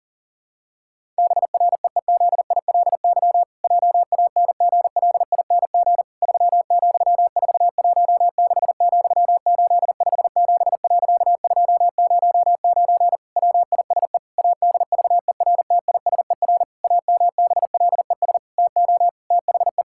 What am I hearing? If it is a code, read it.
Morse code, heard as a keyed tone, about 40 wpm, 700 Hz. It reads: BREEZILY JANGLING 3,416,857.209 WISE ADVERTISER AMBLES TO THE
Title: Pangram Challenge v2 - 40wpm